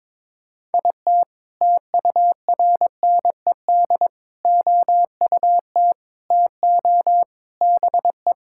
Morse code, heard as a keyed tone, about 22 wpm, 700 Hz. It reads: IT TURNED OUT TO BE
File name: practice / 22wpm / Sentences with 3-Word Phrases - 22wpm